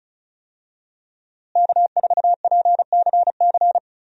Morse code, heard as a keyed tone, about 35 wpm, 700 Hz. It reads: K4PCC
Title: USA Call Signs - 5 Characters - Repeated 3x - 35wpm